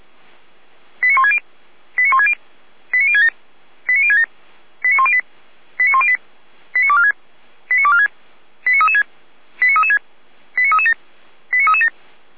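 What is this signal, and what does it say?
Five-tone selective call (ZVEI), heard as quick runs of five tones, tones of about 80 ms. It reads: 89160 89160 89067 89067 89108 89108 89246 89246 89336 89336 89337 89337
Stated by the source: ZVEI five-tone selective call 89078 received on SDR.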